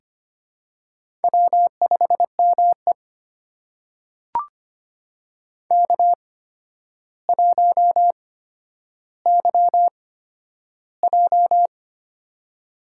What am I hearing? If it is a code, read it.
Morse code, heard as a keyed tone, about 25 wpm, spaced as for 11 wpm, 700 Hz. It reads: W5ME K1YJ
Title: USA Call Signs - 4 Characters - 8x Character Spacing - 25wpm